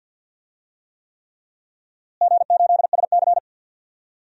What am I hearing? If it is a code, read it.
Morse code, heard as a keyed tone, about 51 wpm, 700 Hz. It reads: G8SC
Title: DXCC-100 Call Signs - 4 Characters - Repeated 3x - 50wpm